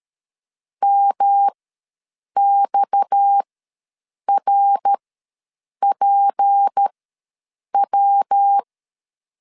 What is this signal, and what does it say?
Morse code, heard as a keyed tone, about 13 wpm, spaced as for 7 wpm, 800 Hz. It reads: MXRPW
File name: LEZIONI / 6 PKX0 / 6 PKX ws 12wpm 3106